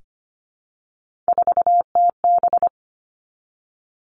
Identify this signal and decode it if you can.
Morse code, heard as a keyed tone, about 25 wpm, 700 Hz. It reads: 4TB
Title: Three Challenging Letter-Number Combinations 25wpm